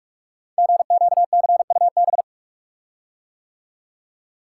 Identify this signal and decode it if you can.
Morse code, heard as a keyed tone, about 46 wpm, 700 Hz. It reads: GQCUB
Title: Code Groups - 5 Letter:Numbers - 45wpm